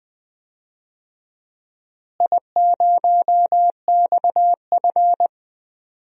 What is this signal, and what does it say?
Morse code, heard as a keyed tone, about 20 wpm, 700 Hz. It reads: I0XF
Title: DXCC-100 Call Signs - 4 Characters - Repeated 2x - 20wpm